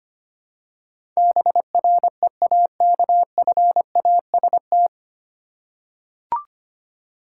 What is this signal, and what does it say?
Morse code, heard as a keyed tone, about 25 wpm, 700 Hz. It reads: BREAKFAST
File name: Nine Letter Words - 25wpm